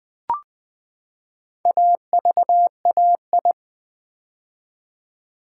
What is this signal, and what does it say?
Morse code, heard as a keyed tone, about 20 wpm, 700 Hz. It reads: AVAI
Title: Build A Word - 4 Letters At A Time - 12-Letter Words - 20wpm